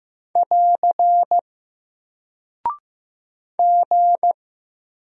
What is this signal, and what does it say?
Morse code, heard as a keyed tone, about 15 wpm, 700 Hz. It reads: <AR> G
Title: Single Character (Letter:Number:Punctuation:Prosign) Alt 15wpm